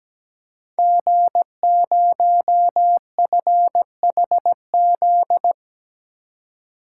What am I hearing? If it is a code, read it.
Morse code, heard as a keyed tone, about 17 wpm, 700 Hz. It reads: G0FHZ
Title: Code Groups - 5 Letter:Numbers - 17wpm